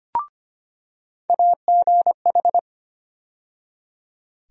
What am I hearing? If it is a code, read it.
Morse code, heard as a keyed tone, about 25 wpm, 700 Hz. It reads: AGH
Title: USA Call Sign Postfixes - 3 Letters 25wpm